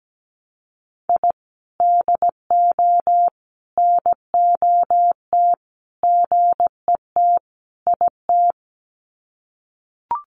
Morse code, heard as a keyed tone, about 17 wpm, 700 Hz. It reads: I DO NOT GET IT
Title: Sets of 5 Words Encore 17wpm